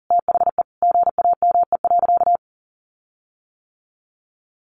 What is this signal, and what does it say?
Morse code, heard as a keyed tone, about 40 wpm, 700 Hz. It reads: THE GAME.
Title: Sentences from Top 1400 Words 40wpm